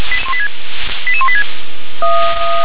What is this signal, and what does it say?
Five-tone selective call (ZVEI), heard as quick runs of five tones, tones of about 70 ms. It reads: 88186
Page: SDR/ZVEI